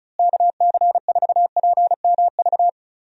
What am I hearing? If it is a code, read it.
Morse code, heard as a keyed tone, about 36 wpm, 700 Hz. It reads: KC4PMV
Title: USA Call Signs - 6 Characters - Repeated 3x - 35wpm